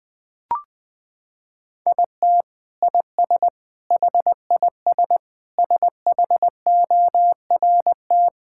Morse code, heard as a keyed tone, about 20 wpm, 700 Hz. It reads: IT IS HIS SHORT
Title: Sentences from Top 1600 Words 20wpm